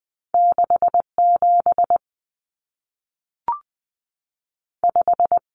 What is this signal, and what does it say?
Morse code, heard as a keyed tone, about 20 wpm, 700 Hz. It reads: 67 5
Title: Common Cut Numbers - 2-Digit 20wpm